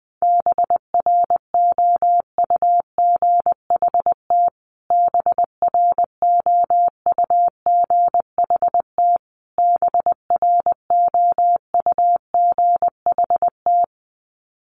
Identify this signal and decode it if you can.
Morse code, heard as a keyed tone, about 20 wpm, 700 Hz. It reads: BROUGHT BROUGHT BROUGHT
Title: Top 500 Words Repeated 20wpm